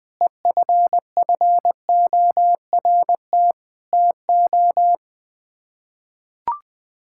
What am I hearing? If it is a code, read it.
Morse code, heard as a keyed tone, about 21 wpm, 700 Hz. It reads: EFFORT TO
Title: Sets of 6 Words - 20wpm